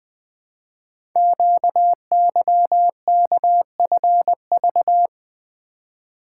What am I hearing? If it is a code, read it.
Morse code, heard as a keyed tone, about 20 wpm, 700 Hz. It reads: QYKFV
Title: Code Groups - 5 Letters - 20wpm